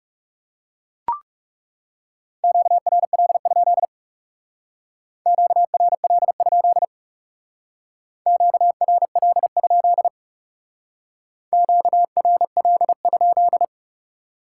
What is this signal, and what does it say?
Morse code, heard as a keyed tone, about 37 wpm, 700 Hz. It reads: QRL? QRL? QRL? QRL?
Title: Speed-Racing Common QSO Phrases - 30wpm